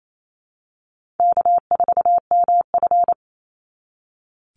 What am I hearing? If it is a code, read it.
Morse code, heard as a keyed tone, about 28 wpm, 700 Hz. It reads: K4MF
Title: NA SOTA Chaser Callsigns 28wpm